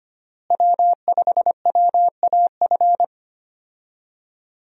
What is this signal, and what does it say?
Morse code, heard as a keyed tone, about 25 wpm, 700 Hz. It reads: W5WAF
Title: USA Call Signs - 5 Characters - Repeated 3x - 25wpm